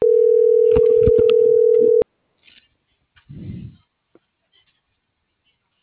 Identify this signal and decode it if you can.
Ringback tone (North America), 440 + 480 Hz.